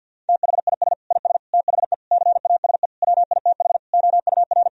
Morse code, heard as a keyed tone, about 50 wpm, 700 Hz. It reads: THIS IS THE CASE WITH OUR
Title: Sentences from Top 1000 Words - Review Part 3 - 50wpm